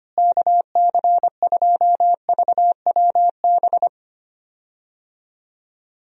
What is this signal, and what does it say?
Morse code, heard as a keyed tone, about 25 wpm, 700 Hz. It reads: KC2VWB
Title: USA Call Signs - 6 Characters - Repeated 3x - 25wpm